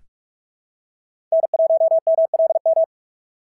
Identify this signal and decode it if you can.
Morse code, heard as a keyed tone, about 44 wpm, 690 Hz. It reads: N1MLM
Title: SOTA QSO Practice 45wpm